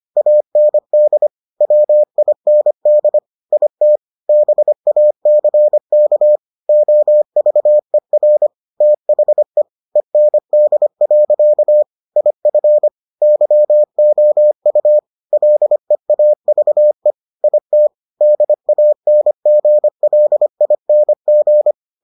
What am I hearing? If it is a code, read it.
Morse code, heard as a keyed tone, about 25 wpm, 600 Hz. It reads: AND WIND IT BACK OVER THE END. IF YOU LEAVE IT DANGLING